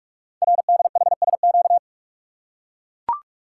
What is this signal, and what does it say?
Morse code, heard as a keyed tone, about 44 wpm, 690 Hz. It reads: RDHSQ
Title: Code Groups - 5 Letters - 45wpm